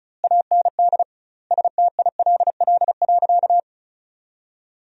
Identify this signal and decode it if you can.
Morse code, heard as a keyed tone, about 34 wpm, 700 Hz. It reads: AND STILL.
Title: 2x Word-Spacing - Sentences from Top 500 Words - 35wpm